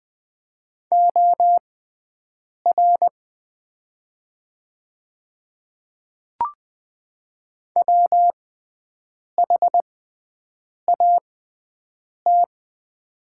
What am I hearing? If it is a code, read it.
Morse code, heard as a keyed tone, about 20 wpm, spaced as for 7 wpm, 700 Hz. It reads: OR WHAT
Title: 6x Character Spacing - Top 300 Words - 20wpm